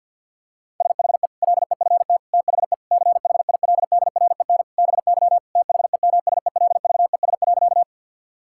Morse code, heard as a keyed tone, about 52 wpm, 700 Hz. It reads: SHE LEFT THE CHILDREN BY THEMSELVES.
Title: Sentences from Top 1000 Words - Review Part 5 - 50wpm